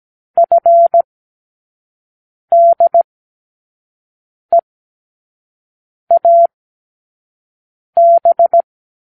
Morse code, heard as a keyed tone, about 17 wpm, 680 Hz. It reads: F D E A B
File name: k7qo / Lesson9(A-F)